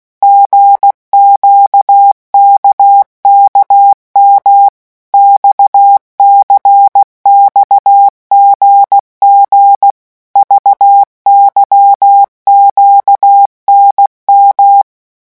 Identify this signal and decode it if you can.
Morse code, heard as a keyed tone, about 16 wpm, 800 Hz. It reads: GQKKM XCXGG VYQNM
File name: morse mp3 / revision3-vitesse 15 mots